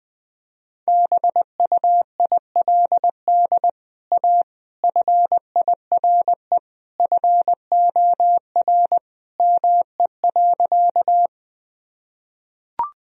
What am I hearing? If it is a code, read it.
Morse code, heard as a keyed tone, about 20 wpm, 700 Hz. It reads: BUILD A FIRE FOR ME.
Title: Sentences from Top 1000 Words - Review Part 1 - 20wpm